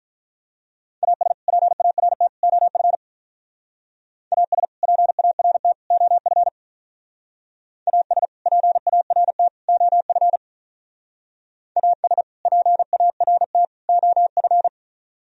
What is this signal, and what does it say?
Morse code, heard as a keyed tone, about 44 wpm, 700 Hz. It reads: AS PART OF AS PART OF AS PART OF AS PART OF